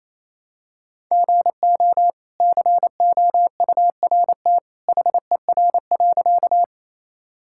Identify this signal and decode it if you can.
Morse code, heard as a keyed tone, about 28 wpm, 700 Hz. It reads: GO COURT HER.
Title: Sentences from Top 1600 Words 28wpm